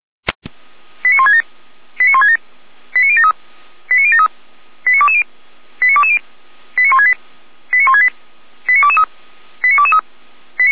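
Five-tone selective call (ZVEI), heard as quick runs of five tones, tones of about 70 ms. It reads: 89167 89167 89073 89073 89220 89220 89168 89168 89223 89223
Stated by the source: Five-tone ZVEI selective call received by SDR, feuerwehr-orlamuende.org fire brigade.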